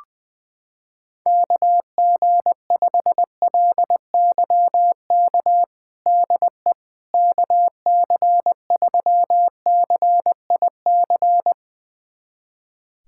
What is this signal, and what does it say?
Morse code, heard as a keyed tone, about 20 wpm, 700 Hz. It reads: KG5LYK DE KC3CIC
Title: USA Callsign DE Callsign - 6 Characters - 20wpm